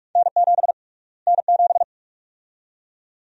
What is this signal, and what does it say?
Morse code, heard as a keyed tone, about 42 wpm, 700 Hz. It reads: N7 N7
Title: USA Call Sign Prefixes - Repeated 2x - 45wpm